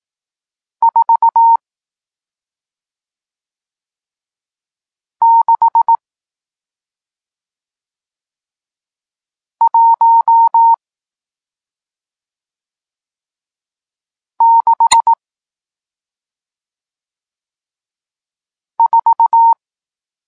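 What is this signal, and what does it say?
Morse code, heard as a keyed tone, about 18 wpm, spaced as for 2 wpm, 920 Hz. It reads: 46164
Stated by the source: Morse code lesson teaching digits 1, 4 and 6, from SP9MOA, Niepołomicki Klub Krótkofalowców (Poland) amateur radio club.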